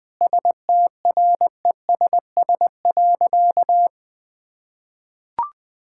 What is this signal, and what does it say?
Morse code, heard as a keyed tone, about 21 wpm, 700 Hz. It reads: STRESS.
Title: Difficult Sentences with Hints - 20wpm